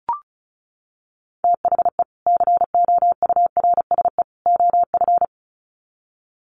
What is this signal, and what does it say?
Morse code, heard as a keyed tone, about 34 wpm, 700 Hz. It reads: THE COURSE OF
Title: Sets of 3 Words 35wpm